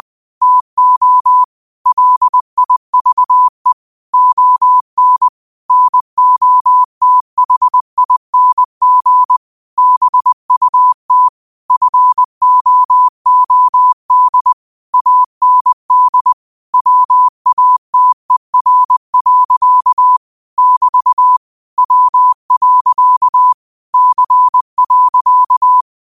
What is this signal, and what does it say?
Morse code, heard as a keyed tone, about 20 wpm, 1000 Hz. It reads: TO LIVE ON NOTHING BUT FOOD AND WATER. <BT> W. C.